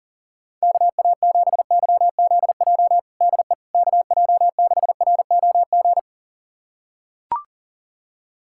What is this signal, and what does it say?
Morse code, heard as a keyed tone, about 40 wpm, 700 Hz. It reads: KA7YZJ DE KJ6ROG
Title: USA Callsign DE Callsign - 6 Characters - 40wpm